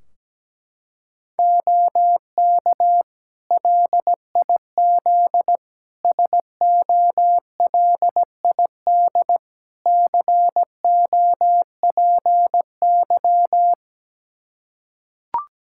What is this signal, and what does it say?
Morse code, heard as a keyed tone, about 17 wpm, 700 Hz. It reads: OK LIZ SOLID COPY
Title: K1USN SST QSO Practice 17wpm